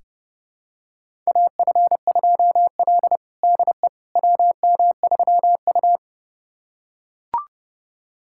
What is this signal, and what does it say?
Morse code, heard as a keyed tone, about 30 wpm, 700 Hz. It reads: AF2L DE WM3U
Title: USA Callsign DE Callsign - 4 Characters - 30wpm